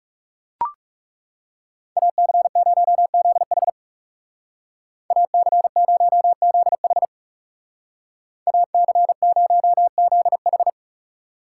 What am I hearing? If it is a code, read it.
Morse code, heard as a keyed tone, about 36 wpm, 700 Hz. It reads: AC0ZH AC0ZH AC0ZH
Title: Speed Racing - USA 2x2 Call Signs 30wpm